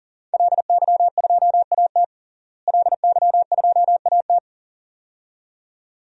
Morse code, heard as a keyed tone, about 40 wpm, 700 Hz. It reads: LY2AT LY2AT
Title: DXCC-100 Call Signs - 5 Characters - Repeated 2x - 40wpm